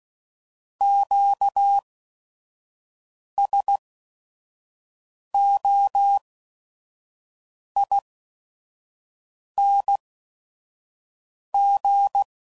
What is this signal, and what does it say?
Morse code, heard as a keyed tone, about 16 wpm, spaced as for 5 wpm, 800 Hz. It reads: QSOING
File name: Suffixes / Special ING's / VerySlowToFaster ING's